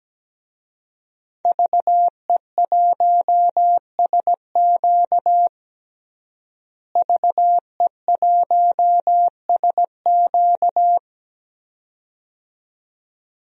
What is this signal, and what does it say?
Morse code, heard as a keyed tone, about 17 wpm, 700 Hz. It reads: VE1SQ VE1SQ
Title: DXCC-100 Call Signs - 5 Characters - Repeated 2x - 17wpm